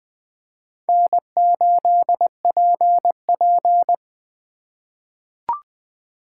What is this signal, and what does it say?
Morse code, heard as a keyed tone, about 20 wpm, 700 Hz. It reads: N8PP
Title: USA Call Signs - 4 Characters - 4x Character Spacing - 20wpm